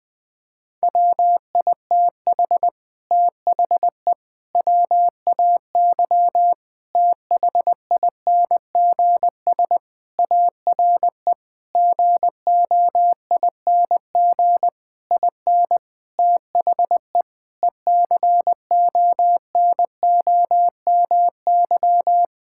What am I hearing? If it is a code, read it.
Morse code, heard as a keyed tone, about 20 wpm, 700 Hz. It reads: WITH THE WAY THINGS ARE GOING IN THE ECONOMY
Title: Sentences with 8-Word Phrases - 20wpm